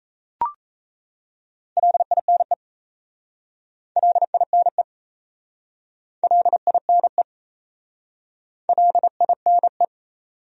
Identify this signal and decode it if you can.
Morse code, heard as a keyed tone, about 39 wpm, 700 Hz. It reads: LINE LINE LINE LINE